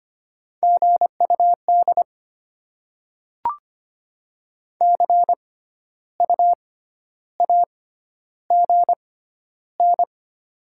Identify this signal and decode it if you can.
Morse code, heard as a keyed tone, about 25 wpm, spaced as for 12 wpm, 700 Hz. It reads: GUD CU AGN
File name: Top 20 QSO Elements - 6x Character Spacing - 25wpm